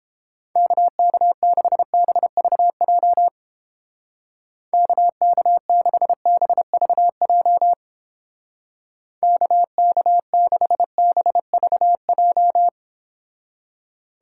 Morse code, heard as a keyed tone, about 29 wpm, 700 Hz. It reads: KK6BVJ KK6BVJ KK6BVJ